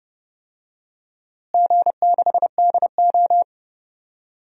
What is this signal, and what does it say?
Morse code, heard as a keyed tone, about 30 wpm, 700 Hz. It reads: G6DO
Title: DXCC-100 Call Signs - 4 Characters - Repeated 2x - 30wpm